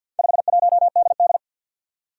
Morse code, heard as a keyed tone, about 52 wpm, 700 Hz. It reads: 51DD
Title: Four Challenging Letter-Number Combinations 50wpm